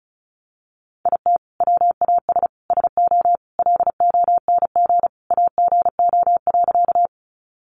Morse code, heard as a keyed tone, about 34 wpm, 700 Hz. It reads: IT WAS SO LONG AGO.
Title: Sentences from Top 500 Words 35wpm